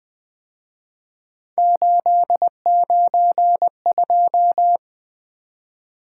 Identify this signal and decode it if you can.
Morse code, heard as a keyed tone, about 25 wpm, 700 Hz. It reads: 892
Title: Speed-Racing - Three Letter-Number Combinations - 20wpm